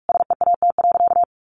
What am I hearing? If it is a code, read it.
Morse code, heard as a keyed tone, about 44 wpm, 700 Hz. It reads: SEAT.
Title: Sentences from Top 1500 Words 45wpm